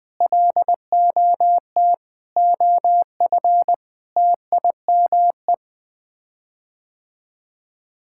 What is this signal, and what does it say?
Morse code, heard as a keyed tone, about 20 wpm, 700 Hz. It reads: LOT OF TIME
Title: Sets of 6 Words - 20wpm